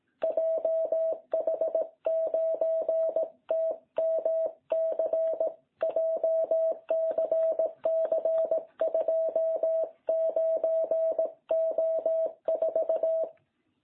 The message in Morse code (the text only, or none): JH9TMC JCC 29O4